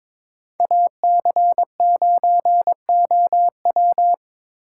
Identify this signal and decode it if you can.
Morse code, heard as a keyed tone, about 22 wpm, 700 Hz. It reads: AC9OW
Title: USA Call Signs - 5 Characters - 4x Character Spacing - 22wpm